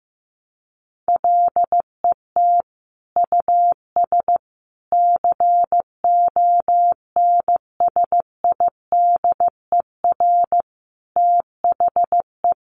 Morse code, heard as a keyed tone, about 15 wpm, 700 Hz. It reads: LET US CONSIDER THE
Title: Sentences from Top 1300 Words 15wpm